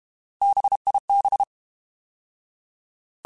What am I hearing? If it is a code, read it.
Morse code, heard as a keyed tone, about 32 wpm, 800 Hz. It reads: BIB
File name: Alphabetical / Alphabetical mp3s / Bs @30